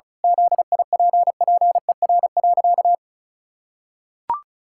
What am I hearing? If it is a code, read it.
Morse code, heard as a keyed tone, about 34 wpm, 700 Hz. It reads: ZIPPER.
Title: Pangram Challenge v2 - 35wpm